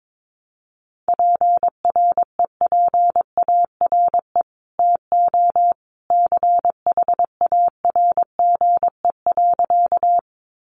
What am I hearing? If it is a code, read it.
Morse code, heard as a keyed tone, about 22 wpm, 700 Hz. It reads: PREPARE TO CHARGE.
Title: Sentences from Top 900 Words 22wpm